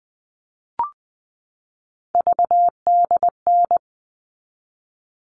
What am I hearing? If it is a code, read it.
Morse code, heard as a keyed tone, about 20 wpm, 700 Hz. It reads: VDN
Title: All Cut Numbers - 3-Digit 20wpm